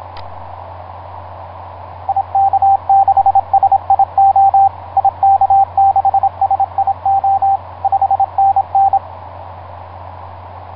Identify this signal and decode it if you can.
Morse code, heard as a keyed tone, about 25 wpm, 780 Hz, with radio noise. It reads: IK6SIO IK6SIO 5NN
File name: vk9x-g6ay ha7lw etc 10m cw 23feb2010 0836